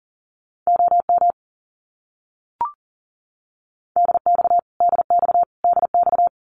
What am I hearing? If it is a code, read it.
Morse code, heard as a keyed tone, about 40 wpm, 700 Hz. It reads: OM DX DX DX
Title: Abbreviations Repeated 3x - 40wpm